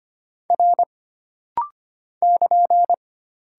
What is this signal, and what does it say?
Morse code, heard as a keyed tone, about 25 wpm, 700 Hz. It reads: R <KN>
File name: Single Character (Letter:Number:Punctuation:Prosign) - Rapid-Fire - 25wpm